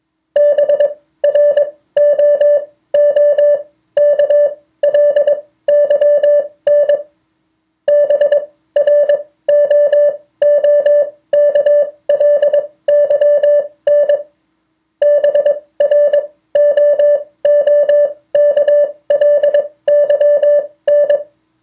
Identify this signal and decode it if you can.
Morse code, heard as a keyed tone, about 22 wpm, spaced as for 16 wpm, 580 Hz. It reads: BROOKLYN BROOKLYN BROOKLYN